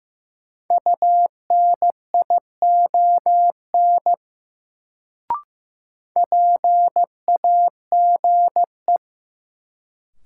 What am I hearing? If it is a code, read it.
Morse code, heard as a keyed tone, about 15 wpm, 700 Hz. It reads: UNION PAGE
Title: Sets of 5 Words Encore - New Word Drill - 15wpm